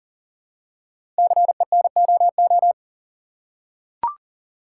Morse code, heard as a keyed tone, about 39 wpm, 700 Hz. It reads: CENOO
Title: Code Groups - 5 Letters - 40wpm